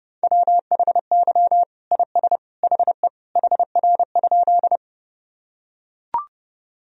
Morse code, heard as a keyed tone, about 30 wpm, 700 Hz. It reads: WHY IS HE HR?
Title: Sentences from Top 200 Words and Abbreviations 30wpm